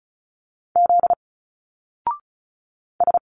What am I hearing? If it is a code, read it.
Morse code, heard as a keyed tone, about 36 wpm, 700 Hz. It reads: Z S
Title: Single Letters 35wpm